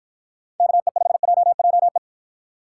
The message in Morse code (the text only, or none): KE5JJE